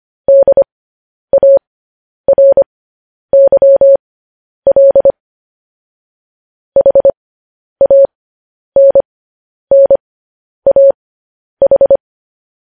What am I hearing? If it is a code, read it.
Morse code, heard as a keyed tone, about 25 wpm, spaced as for 10 wpm, 550 Hz. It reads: DARYL HANNAH
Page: news headlines in morse code at 10 wpm